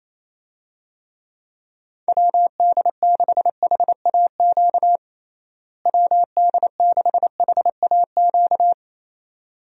Morse code, heard as a keyed tone, about 28 wpm, 700 Hz. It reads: WD6HAQ WD6HAQ
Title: USA Call Signs - 6 Characters - Repeated 2x - 28wpm